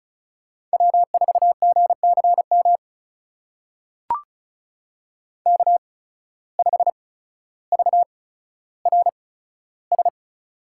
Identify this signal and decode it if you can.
Morse code, heard as a keyed tone, about 36 wpm, spaced as for 15 wpm, 700 Hz. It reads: W4GCM K5VRS
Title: USA Call Signs - 5 Characters - 8x Character Spacing - 35wpm